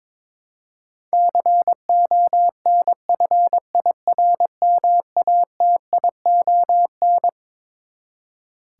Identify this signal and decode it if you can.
Morse code, heard as a keyed tone, about 22 wpm, 700 Hz. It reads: CONFIRMATION
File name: Twelve Letter Words - 22wpm